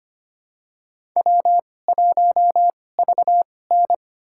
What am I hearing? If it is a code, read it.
Morse code, heard as a keyed tone, about 25 wpm, spaced as for 18 wpm, 700 Hz. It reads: W1VN
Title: USA Call Signs - 4 Characters - 2x Character Spacing - 25wpm